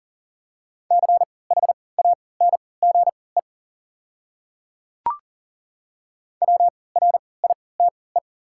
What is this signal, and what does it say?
Morse code, heard as a keyed tone, about 40 wpm, spaced as for 23 wpm, 690 Hz. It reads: CHANGE WRITE
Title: 3x Character Spacing - Top 200 Words - 40wpm